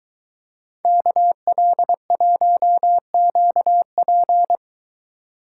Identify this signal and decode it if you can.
Morse code, heard as a keyed tone, about 22 wpm, 700 Hz. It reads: KL1QP